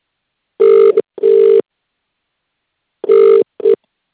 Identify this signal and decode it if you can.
Ringback tone (UK), 400 + 450 Hz.